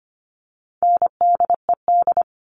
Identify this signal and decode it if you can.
Morse code, heard as a keyed tone, about 25 wpm, 700 Hz. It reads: NDED
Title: All Cut Numbers - 4-Digit 25wpm